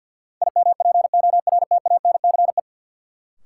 Morse code, heard as a keyed tone, about 53 wpm, 700 Hz. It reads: IMPORTANCE
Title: Sets of 3 Words - New Word Drill - 50wpm